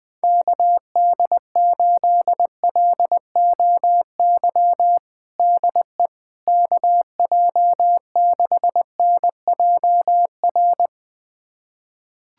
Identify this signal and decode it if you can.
Morse code, heard as a keyed tone, about 20 wpm, 700 Hz. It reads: KD8LOY DE KJ6NJR
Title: USA Callsign DE Callsign - 6 Characters - 20wpm